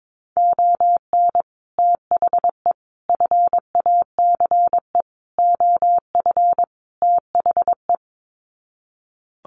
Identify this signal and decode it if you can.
Morse code, heard as a keyed tone, about 22 wpm, 700 Hz. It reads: ON THE FACE OF THE
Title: Sets of 5 Words Encore2 22wpm